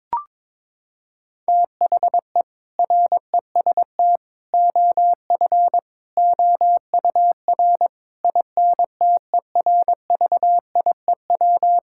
Morse code, heard as a keyed tone, about 22 wpm, 700 Hz. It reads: THE REST OF OUR INTERVIEW
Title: Sentences from Top 1600 Words 22wpm